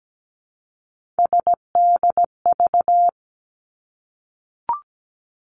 Morse code, heard as a keyed tone, about 17 wpm, 700 Hz. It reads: SDV
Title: Three Challenging Letter-Number Combinations 17wpm